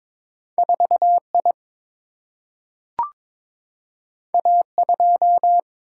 Morse code, heard as a keyed tone, about 22 wpm, 700 Hz. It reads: 4 I A 2
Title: Single-Letter Single-Number 22wpm